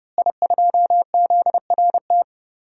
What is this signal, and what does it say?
Morse code, heard as a keyed tone, about 30 wpm, 700 Hz. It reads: I2ZRT
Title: DXCC-100 Call Signs - 5 Characters - Repeated 3x - 30wpm